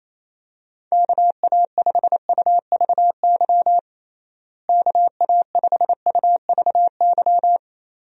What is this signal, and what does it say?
Morse code, heard as a keyed tone, about 28 wpm, 700 Hz. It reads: KA5UVY KA5UVY
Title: USA Call Signs - 6 Characters - Repeated 2x - 28wpm